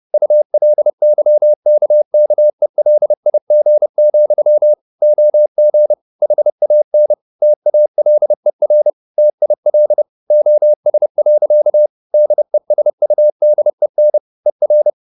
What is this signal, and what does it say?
Morse code, heard as a keyed tone, about 30 wpm, 600 Hz. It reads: ULYKKELIG, OG HAN TALER TIL OS. DESUDEN ER
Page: Ferie 30 wpm